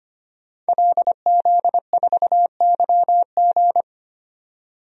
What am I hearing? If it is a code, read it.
Morse code, heard as a keyed tone, about 25 wpm, 700 Hz. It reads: LZ4YG
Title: DXCC-100 Call Signs - 5 Characters - 25wpm